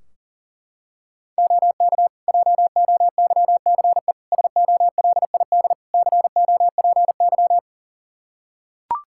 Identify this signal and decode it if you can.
Morse code, heard as a keyed tone, about 40 wpm, 700 Hz. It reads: OK JOYCE SOLID COPY
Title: Practice QSO - Exchange 3 - More Info - 40wpm